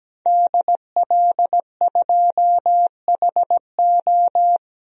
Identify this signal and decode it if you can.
Morse code, heard as a keyed tone, about 17 wpm, 700 Hz. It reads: DL2HO
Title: DXCC-100 Call Signs - 5 Characters - Repeated 2x - 17wpm